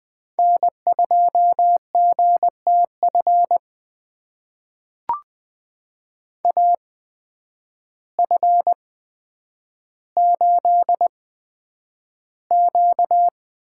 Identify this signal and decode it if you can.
Morse code, heard as a keyed tone, about 20 wpm, spaced as for 9 wpm, 700 Hz. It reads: N2GTF AF8Q
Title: USA Call Signs - 8x Character Spacing - 20wpm